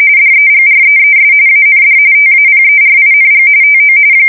RTTY, about 45.45 baud. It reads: DEGREES AND TAPERING TO
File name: fsk170 / battleshipnj